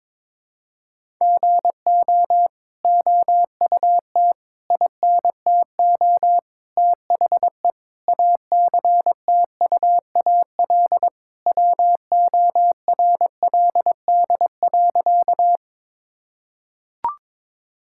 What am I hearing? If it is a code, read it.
Morse code, heard as a keyed tone, about 22 wpm, 700 Hz. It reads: GO OUT INTO THE ACTUAL WORLD.
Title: Sentences from Top 1000 Words - Review Part 5 - 22wpm